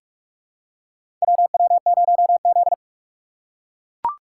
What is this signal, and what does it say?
Morse code, heard as a keyed tone, about 43 wpm, 690 Hz. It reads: WW0Z
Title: USA Call Signs - 3x Character Spacing - 45wpm